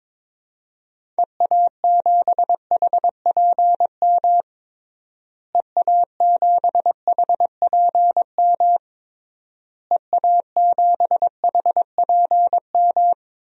Morse code, heard as a keyed tone, about 22 wpm, 700 Hz. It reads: EA7HPM EA7HPM EA7HPM
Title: DXCC-100 Call Signs - 6 Characters - Repeated 3x - 22wpm